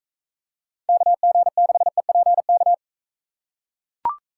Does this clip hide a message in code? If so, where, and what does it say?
Yes: Morse code from 0.9 to 2.7 s, heard as a keyed tone, about 42 wpm, 700 Hz. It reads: KG6EPK